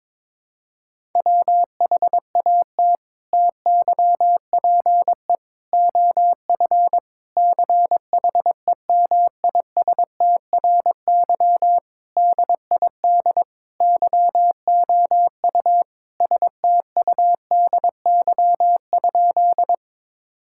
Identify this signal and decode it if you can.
Morse code, heard as a keyed tone, about 22 wpm, 700 Hz. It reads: WHAT TYPE OF CHEMISTRY DID YOU STUDY?
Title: Sentences from Top 1000 Words - Review Part 6 - 22wpm